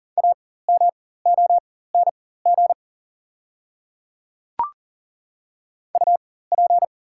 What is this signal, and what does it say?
Morse code, heard as a keyed tone, about 40 wpm, spaced as for 19 wpm, 700 Hz. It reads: AMONG UP
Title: 4x Character Spacing - Top 500 Words - 40wpm